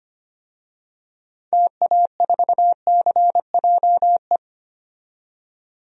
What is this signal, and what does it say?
Morse code, heard as a keyed tone, about 25 wpm, 700 Hz. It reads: TA4CJE
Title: DXCC-100 Call Signs - 6 Characters - 25wpm